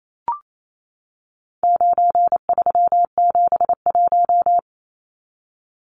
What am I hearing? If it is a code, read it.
Morse code, heard as a keyed tone, about 28 wpm, 700 Hz. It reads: 9371
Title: Numbers - 4-Digit 28wpm